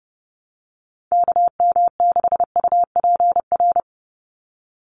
Morse code, heard as a keyed tone, about 30 wpm, 700 Hz. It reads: KM6UPR
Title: USA Call Signs 30wpm